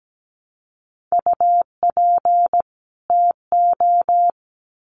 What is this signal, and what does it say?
Morse code, heard as a keyed tone, about 17 wpm, 700 Hz. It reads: UP TO
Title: Sets of 2 Words 17wpm